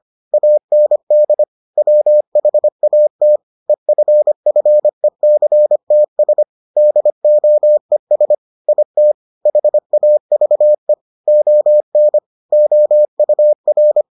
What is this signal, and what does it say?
Morse code, heard as a keyed tone, about 25 wpm, 600 Hz. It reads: AND WHAT EFFECTS DOES IT HAVE ON OUR